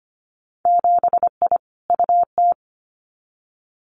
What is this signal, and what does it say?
Morse code, heard as a keyed tone, about 25 wpm, 700 Hz. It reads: 7I UT
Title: Winter Field Day Exchanges Encore 25wpm